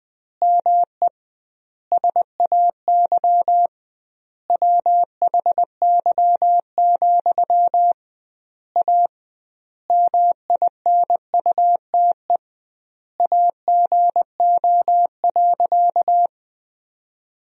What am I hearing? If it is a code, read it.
Morse code, heard as a keyed tone, about 20 wpm, 700 Hz. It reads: ME SAY WHY, A MINUTE AGO.
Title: 2x Word-Spacing - Sentences from Top 500 Words - 20wpm